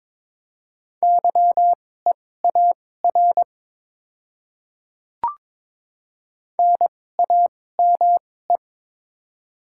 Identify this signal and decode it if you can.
Morse code, heard as a keyed tone, about 22 wpm, spaced as for 16 wpm, 700 Hz. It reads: YEAR NAME
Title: 2x Character Spacing - Top 200 Words - 22wpm